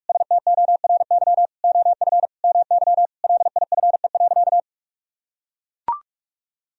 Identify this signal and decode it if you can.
Morse code, heard as a keyed tone, about 41 wpm, 700 Hz. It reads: STORY OF MY LIFE.
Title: Sentences from Top 1000 Words - Review Part 1 - 45wpm